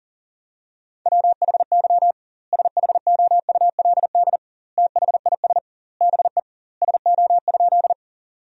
Morse code, heard as a keyed tone, about 40 wpm, 700 Hz. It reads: WHY SHOULD THIS BE SO?
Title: 2x Word-Spacing - Sentences from Top 300 Words - 40wpm